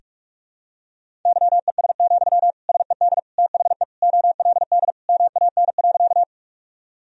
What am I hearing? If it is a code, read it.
Morse code, heard as a keyed tone, about 46 wpm, 700 Hz. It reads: YES, SED THE OLD MAN.
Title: Sentences from Top 500 Words and Abbreviations 45wpm